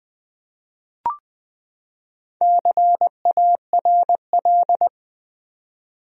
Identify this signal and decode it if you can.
Morse code, heard as a keyed tone, about 20 wpm, 700 Hz. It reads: CARL
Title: USA Common Names - Max 5 Letters - 20wpm